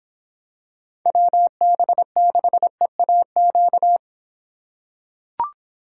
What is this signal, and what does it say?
Morse code, heard as a keyed tone, about 26 wpm, 700 Hz. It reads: WB6EAQ